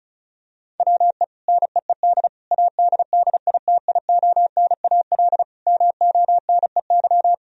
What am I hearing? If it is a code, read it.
Morse code, heard as a keyed tone, about 36 wpm, 700 Hz. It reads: WE NEED ADDITIONAL MONEY
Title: Sentences from Top 1000 Words - Review Part 4 - 35wpm